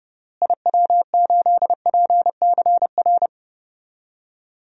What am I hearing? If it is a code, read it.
Morse code, heard as a keyed tone, about 32 wpm, 700 Hz. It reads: IW8PCR
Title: DXCC-100 Call Signs - 6 Characters - Repeated 3x - 30wpm